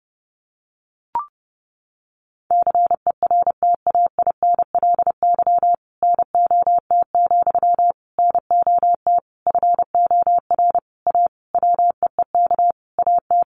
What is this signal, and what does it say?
Morse code, heard as a keyed tone, about 30 wpm, 700 Hz. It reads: CERTAINLY NOT, NOT FOR A WEEK AT
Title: Sentences from Top 900 Words 30wpm